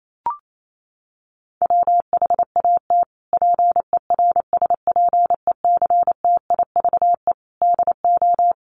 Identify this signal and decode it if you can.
Morse code, heard as a keyed tone, about 28 wpm, 700 Hz. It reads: WHAT PERSPECTIVE DO
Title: Sentences from Top 1200 Words 28wpm